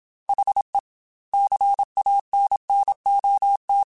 Morse code, heard as a keyed tone, about 27 wpm, 800 Hz. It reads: HE CANNOT